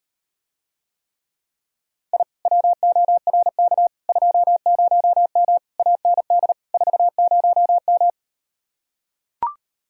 Morse code, heard as a keyed tone, about 38 wpm, 700 Hz. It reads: I WORK 20M AND 40M